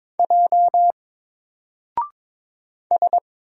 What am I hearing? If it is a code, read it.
Morse code, heard as a keyed tone, about 22 wpm, 700 Hz. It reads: J S
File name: Single-Letter-Number-22wpm